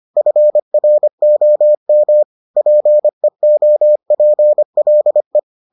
Morse code, heard as a keyed tone, about 25 wpm, 600 Hz. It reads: FROM PEOPLE